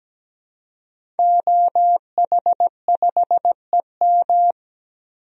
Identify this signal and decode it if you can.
Morse code, heard as a keyed tone, about 17 wpm, 700 Hz. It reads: OH5EM
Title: DXCC-100 Call Signs - 5 Characters - Repeated 3x - 17wpm